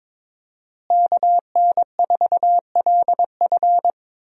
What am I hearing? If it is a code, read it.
Morse code, heard as a keyed tone, about 22 wpm, 700 Hz. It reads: KN4LF
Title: USA Call Signs - 5 Characters - Repeated 2x - 22wpm